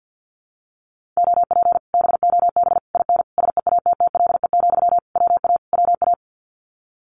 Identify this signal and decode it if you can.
Morse code, heard as a keyed tone, about 48 wpm, 700 Hz. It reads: OP BOB IN SEATTLE, WA WA
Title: Common QSO Phrases Encore 50wpm